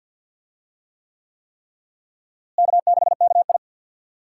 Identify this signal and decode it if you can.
Morse code, heard as a keyed tone, about 53 wpm, 700 Hz. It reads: K6KI
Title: USA Call Signs - 4 Characters - Repeated 3x - 50wpm